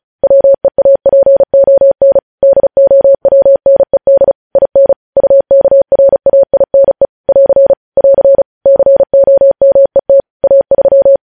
Morse code, heard as a keyed tone, about 36 wpm, 550 Hz. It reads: WEAPON DOWNED IN UKRAINE <AR> <AR> COMET A3